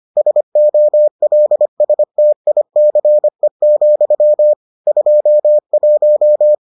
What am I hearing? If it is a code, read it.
Morse code, heard as a keyed tone, about 25 wpm, 600 Hz. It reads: SOLSTICE, 21